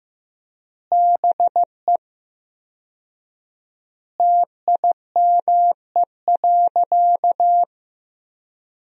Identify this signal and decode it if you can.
Morse code, heard as a keyed tone, about 15 wpm, 700 Hz. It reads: BE TIME.
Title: 4x Word-Spacing - Sentences from Top 200 Words - 15wpm